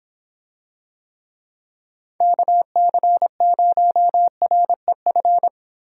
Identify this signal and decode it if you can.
Morse code, heard as a keyed tone, about 26 wpm, 700 Hz. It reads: KC0REF